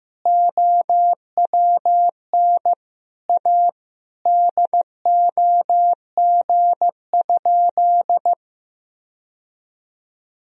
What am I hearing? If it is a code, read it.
Morse code, heard as a keyed tone, about 15 wpm, 700 Hz. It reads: OWN A DOG?
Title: Sentences from Top 1000 Words - Review Part 2 - 15wpm